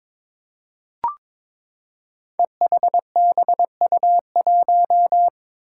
Morse code, heard as a keyed tone, about 22 wpm, 700 Hz. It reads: EHBU1
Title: Five Challenging Letter-Number Combinations 22wpm